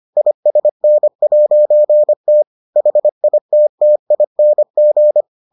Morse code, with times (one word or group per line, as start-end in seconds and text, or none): 0.2-2.4 ISN'T
2.8-5.2 HITTING